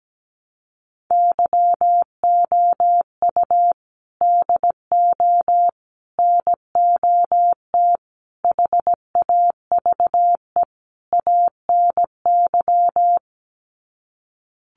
Morse code, heard as a keyed tone, about 17 wpm, 700 Hz. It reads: YOU DO NOT HAVE ANY
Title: Sets of 5 Words Encore 17wpm